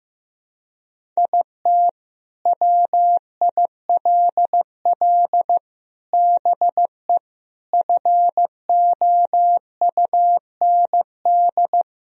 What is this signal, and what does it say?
Morse code, heard as a keyed tone, about 15 wpm, 700 Hz. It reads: IT WILL BE FOUND
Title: Sentences from Top 1000 Words - Review Part 1 - 15wpm